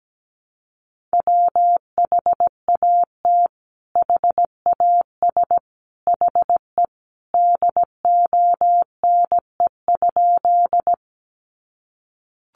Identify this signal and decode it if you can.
Morse code, heard as a keyed tone, about 17 wpm, 700 Hz. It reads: WHAT HAS HE DONE?
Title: Sentences from Top 500 Words 17wpm